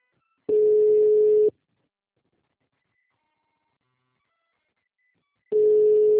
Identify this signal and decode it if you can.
Ringback tone (Europe, 425 Hz).
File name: out- -2002- - - .4948